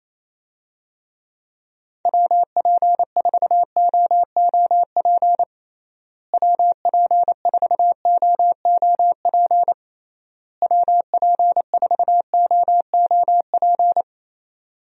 Morse code, heard as a keyed tone, about 27 wpm, 700 Hz. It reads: WP4OOP WP4OOP WP4OOP
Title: USA Call Signs - 6 Characters - Repeated 3x - 28wpm